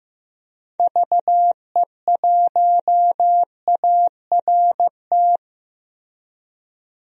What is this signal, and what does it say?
Morse code, heard as a keyed tone, about 15 wpm, 700 Hz. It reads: VE1ART
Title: DXCC-100 Call Signs - 6 Characters - Repeated 3x - 15wpm